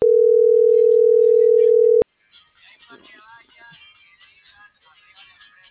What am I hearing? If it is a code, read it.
Ringback tone (North America), 440 + 480 Hz.